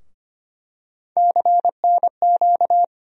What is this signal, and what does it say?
Morse code, heard as a keyed tone, about 25 wpm, 700 Hz. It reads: CNQ
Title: USA Call Sign Postfixes - 3 Letters 25wpm